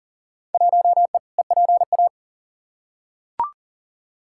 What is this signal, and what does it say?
Morse code, heard as a keyed tone, about 40 wpm, 700 Hz. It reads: 1E EPA
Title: Field Day Exchanges - 40wpm